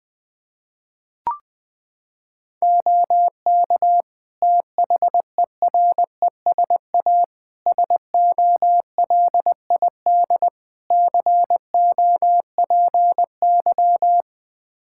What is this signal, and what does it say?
Morse code, heard as a keyed tone, about 20 wpm, 700 Hz. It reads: OK THERESA SOLID COPY
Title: Practice QSO - Exchange 3 - More Info - 20wpm